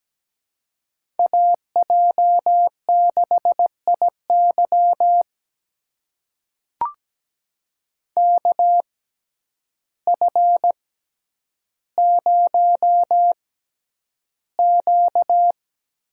Morse code, heard as a keyed tone, about 17 wpm, spaced as for 9 wpm, 700 Hz. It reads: AJ6IY KF0Q
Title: USA Call Signs - 6x Character Spacing - 17wpm